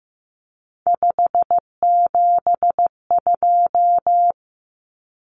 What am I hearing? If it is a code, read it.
Morse code, heard as a keyed tone, about 15 wpm, 700 Hz. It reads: 572
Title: Common Cut Numbers - 3-Digit 15wpm